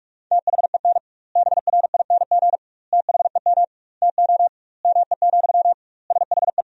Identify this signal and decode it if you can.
Morse code, heard as a keyed tone, about 46 wpm, 700 Hz. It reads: THEN BRING THEM TO ME, SHE
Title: 2x Word-Spacing - Sentences from Top 500 Words - 45wpm